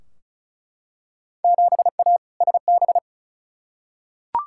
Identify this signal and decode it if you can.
Morse code, heard as a keyed tone, about 36 wpm, 690 Hz. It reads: 7A SB
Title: Field Day Exchanges Encore - 35wpm